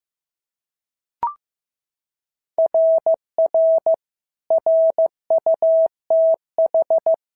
Morse code, heard as a keyed tone, about 14 wpm, 650 Hz. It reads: RR RUTH
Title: SOTA QSO Practice 15wpm